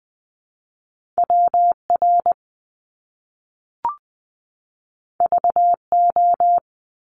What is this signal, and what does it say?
Morse code, heard as a keyed tone, about 20 wpm, 700 Hz. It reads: WR VO
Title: Top Two Letter Combinations in Words 20wpm